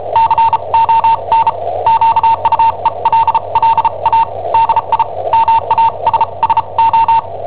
Morse code, heard as a keyed tone, about 34 wpm, spaced as for 29 wpm, 910 Hz. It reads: CON QUELLA DI MASSO
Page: Cassetta esercizi preparata da I7OHP per IW5ELP